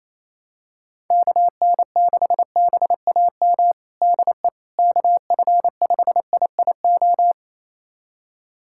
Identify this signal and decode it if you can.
Morse code, heard as a keyed tone, about 27 wpm, 700 Hz. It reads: KN6BAM DE KF5IIO
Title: USA Callsign DE Callsign - 6 Characters - 28wpm